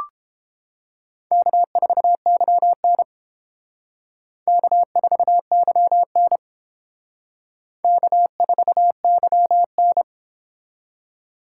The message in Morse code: K4YN K4YN K4YN

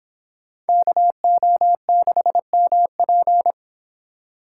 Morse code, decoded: KO6MP

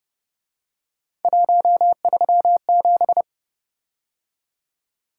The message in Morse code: 137

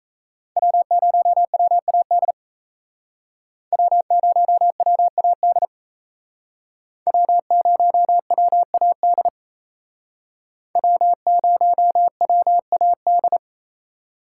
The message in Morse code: W0WAD W0WAD W0WAD W0WAD

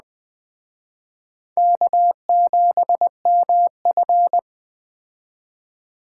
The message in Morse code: K7MF

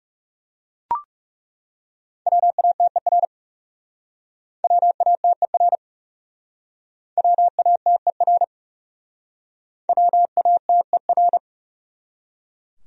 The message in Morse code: WATER WATER WATER WATER

